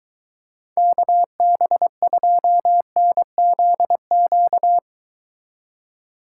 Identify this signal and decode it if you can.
Morse code, heard as a keyed tone, about 23 wpm, 700 Hz. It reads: KB2NZQ